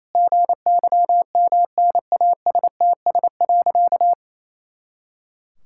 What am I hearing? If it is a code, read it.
Morse code, heard as a keyed tone, about 28 wpm, 700 Hz. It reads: GYMNASTS.
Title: Pangram Challenge v2 - 28wpm